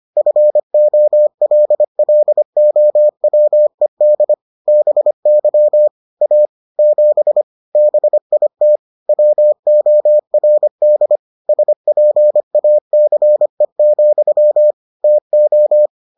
Morse code, heard as a keyed tone, about 25 wpm, 600 Hz. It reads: FOLLOWED BY A 7 BIT WORD SPACE, TO